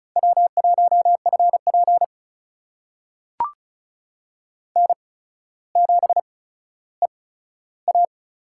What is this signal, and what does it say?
Morse code, heard as a keyed tone, about 36 wpm, spaced as for 15 wpm, 700 Hz. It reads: W1FP N7EA
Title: USA Call Signs - 4 Characters - 8x Character Spacing - 35wpm